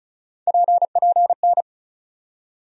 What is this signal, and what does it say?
Morse code, heard as a keyed tone, about 36 wpm, 700 Hz. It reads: PPN
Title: USA Call Sign Postfixes - 3 Letters 35wpm